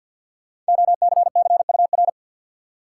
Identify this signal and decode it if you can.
Morse code, heard as a keyed tone, about 53 wpm, 700 Hz. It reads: OCCUR
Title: Sentences from Top 900 Words - New Word Drill - 50wpm